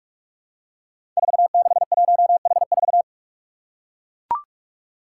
Morse code, heard as a keyed tone, about 44 wpm, 700 Hz. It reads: 461H4
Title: Five Challenging Letter-Number Combinations 45wpm